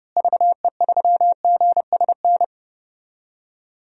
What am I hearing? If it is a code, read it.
Morse code, heard as a keyed tone, about 30 wpm, 700 Hz. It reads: VE3GSN
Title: DXCC-100 Call Signs - 6 Characters - Repeated 2x - 30wpm